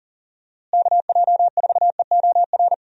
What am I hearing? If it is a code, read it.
Morse code, heard as a keyed tone, about 40 wpm, 700 Hz. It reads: KJ4EOR
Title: USA Call Signs - 6 Characters - Repeated 2x - 40wpm